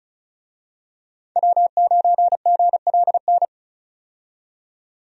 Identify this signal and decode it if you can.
Morse code, heard as a keyed tone, about 36 wpm, 700 Hz. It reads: W9GLN
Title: USA Call Signs - 5 Characters - 35wpm